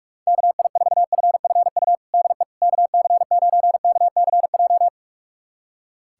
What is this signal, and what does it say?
Morse code, heard as a keyed tone, about 46 wpm, 700 Hz. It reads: KI4FFU DE KC9KCJ
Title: USA Callsign DE Callsign 45wpm